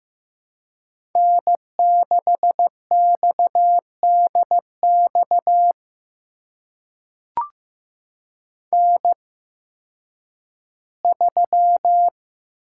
Morse code, heard as a keyed tone, about 15 wpm, spaced as for 10 wpm, 700 Hz. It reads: N6XDX N3
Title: USA Call Signs - 5 Characters - 8x Character Spacing - 15wpm